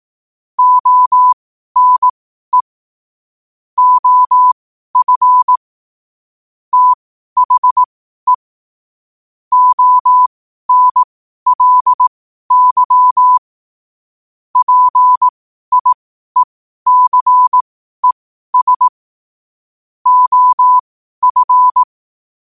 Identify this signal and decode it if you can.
Morse code, heard as a keyed tone, about 18 wpm, spaced as for 12 wpm, 1000 Hz. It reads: ONE OF THE ONLY PIECES OF
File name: qotdfiles / 12wpm